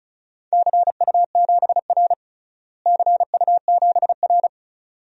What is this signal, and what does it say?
Morse code, heard as a keyed tone, about 36 wpm, 700 Hz. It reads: CU7R CU7R
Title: DXCC-100 Call Signs - 4 Characters - Repeated 2x - 35wpm